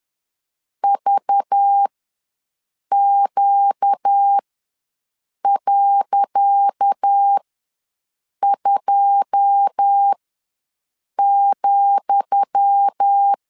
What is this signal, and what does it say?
Morse code, heard as a keyed tone, about 10 wpm, spaced as for 6 wpm, 800 Hz. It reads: VQ.2,
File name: LEZIONI / 10 SIGNS / 10D COMA ws 10wpm 4314